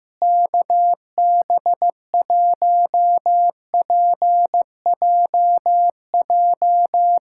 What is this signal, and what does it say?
Morse code, heard as a keyed tone, about 15 wpm, 700 Hz. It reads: KB1PJJ
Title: USA Callsign DE Callsign 15wpm